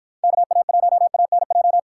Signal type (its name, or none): Morse code